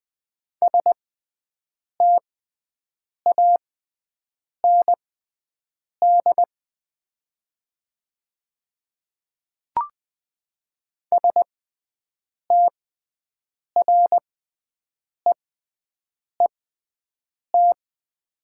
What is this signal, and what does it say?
Morse code, heard as a keyed tone, about 20 wpm, spaced as for 7 wpm, 700 Hz. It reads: STAND STREET
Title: 6x Character Spacing - Top 500 Words - 20wpm